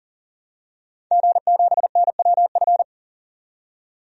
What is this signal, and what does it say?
Morse code, heard as a keyed tone, about 40 wpm, 700 Hz. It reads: G7NWF
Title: DXCC-100 Call Signs - 5 Characters - 40wpm